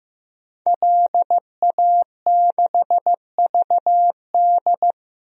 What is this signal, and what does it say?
Morse code, heard as a keyed tone, about 15 wpm, 700 Hz. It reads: LA6VD
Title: DXCC-100 Call Signs - 5 Characters - Repeated 3x - 15wpm